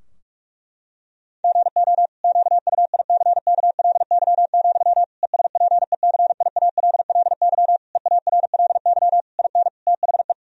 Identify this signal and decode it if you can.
Morse code, heard as a keyed tone, about 46 wpm, 700 Hz. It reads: GO QUICKLY, ESPECIALLY EARLY IN THE
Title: Sentences from Top 1000 Words - Review Part 4 - 45wpm